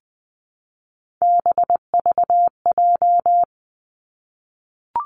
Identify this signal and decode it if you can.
Morse code, heard as a keyed tone, about 20 wpm, 700 Hz. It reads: BVJ
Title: Three Challenging Letter-Number Combinations 20wpm